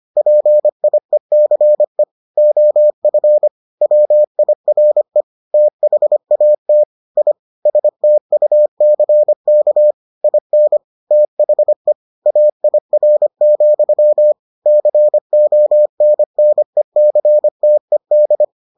Morse code, heard as a keyed tone, about 25 wpm, 600 Hz. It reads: PIECE OF WIRE THAT I STUCK IN THE AIR, CONNECTED